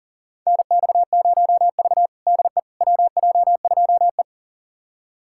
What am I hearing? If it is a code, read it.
Morse code, heard as a keyed tone, about 40 wpm, 700 Hz. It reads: NX0V DE WJ2E
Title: USA Callsign DE Callsign - 4 Characters - 40wpm